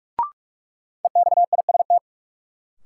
Morse code, heard as a keyed tone, about 46 wpm, 700 Hz. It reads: EXIST
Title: Sentences from Top 1100 Words - New Word Drill Alt - 45wpm